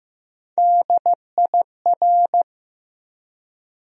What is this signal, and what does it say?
Morse code, heard as a keyed tone, about 15 wpm, 700 Hz. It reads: DIR
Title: USA Call Sign Postfixes - 3 Letters 15wpm